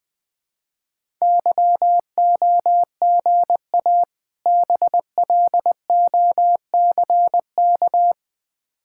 Morse code, heard as a keyed tone, about 21 wpm, 700 Hz. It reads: YOGA BLOCK
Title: Difficult Words with Hints - 20wpm